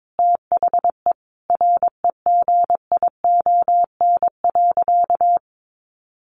Morse code, heard as a keyed tone, about 22 wpm, 700 Hz. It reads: THE REGION.
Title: Sentences from Top 1500 Words 22wpm